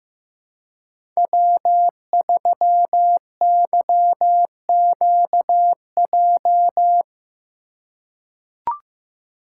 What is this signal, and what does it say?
Morse code, heard as a keyed tone, about 15 wpm, 700 Hz. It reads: W3YQJ
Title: Code Groups - 5 Letter:Numbers - 15wpm